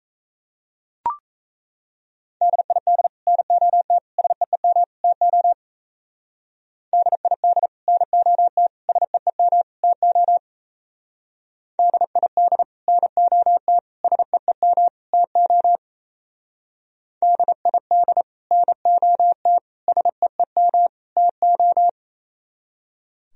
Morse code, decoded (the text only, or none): DID NOT SEEM TO DID NOT SEEM TO DID NOT SEEM TO DID NOT SEEM TO